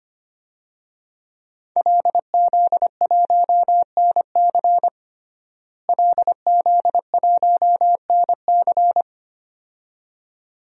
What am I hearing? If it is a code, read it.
Morse code, heard as a keyed tone, about 25 wpm, 700 Hz. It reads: LZ1NC LZ1NC
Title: DXCC-100 Call Signs - 5 Characters - Repeated 2x - 25wpm